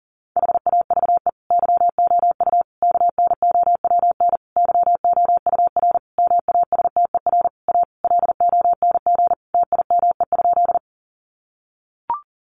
Morse code, heard as a keyed tone, about 40 wpm, 700 Hz. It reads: HAVE YOU KNOWN YOUR MASTER A LONG TIME?
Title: Sentences from Top 1300 Words 40wpm